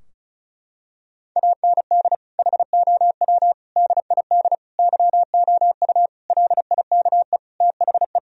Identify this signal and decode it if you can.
Morse code, heard as a keyed tone, about 34 wpm, 700 Hz. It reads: AND HOW DID YOU LIKE THE
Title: Sentences from Top 1000 Words - Review Part 4 - 35wpm